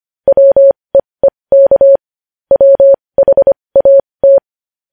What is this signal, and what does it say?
Morse code, heard as a keyed tone, about 25 wpm, spaced as for 20 wpm, 550 Hz. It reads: WEEK WHAT
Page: News Headlines in Morse Code at 20 WPM